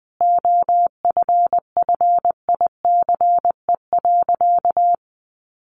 Morse code, heard as a keyed tone, about 20 wpm, 700 Hz. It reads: OFFICE.
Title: Sentences from Top 1500 Words 20wpm